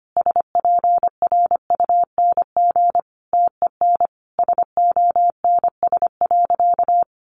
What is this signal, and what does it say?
Morse code, heard as a keyed tone, about 25 wpm, 700 Hz. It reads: SPRUNG TEN SONS.
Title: A Sound Of Thunder 25wpm